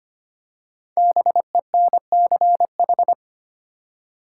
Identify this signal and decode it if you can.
Morse code, heard as a keyed tone, about 25 wpm, 690 Hz. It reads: BENCH
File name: practice / 25wpm / Easy Words with Hints - 25wpm